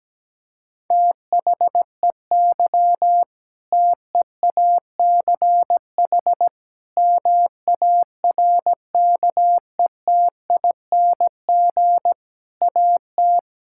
Morse code, heard as a keyed tone, about 17 wpm, 700 Hz. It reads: THEY TEACH MARKETING AT
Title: Sentences from Top 1000 Words - Review Part 5 - 17wpm